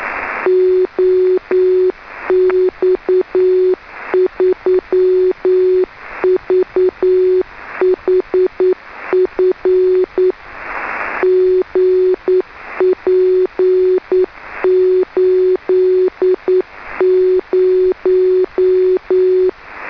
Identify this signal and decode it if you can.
Morse code, heard as a keyed tone, about 9 wpm, 370 Hz. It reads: OX3VHF GP80
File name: 2m beac 144,418 Mhz